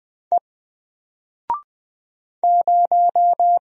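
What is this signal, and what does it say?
Morse code, heard as a keyed tone, about 20 wpm, 700 Hz. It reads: E 0
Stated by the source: Single letter and number Morse drill at 20 wpm, from N4LCD.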